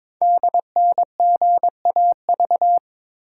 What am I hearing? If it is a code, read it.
Morse code, heard as a keyed tone, about 22 wpm, 700 Hz. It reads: DNGAV
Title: Code Groups - 5 Letters - 22wpm